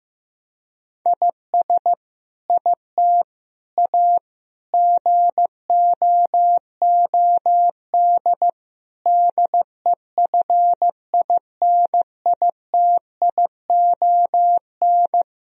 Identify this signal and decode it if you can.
Morse code, heard as a keyed tone, about 15 wpm, 700 Hz. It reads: IS IT A GOOD DEFINITION
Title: Sentences from Top 1000 Words - Review Part 4 - 15wpm